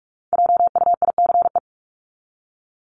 Morse code, heard as a keyed tone, about 44 wpm, 700 Hz. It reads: JUICE
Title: Top 1251-1500 Words 45wpm